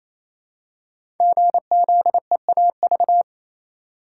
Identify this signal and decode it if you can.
Morse code, heard as a keyed tone, about 27 wpm, 700 Hz. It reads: GZEAV
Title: Code Groups - 5 Letter:Numbers - 28wpm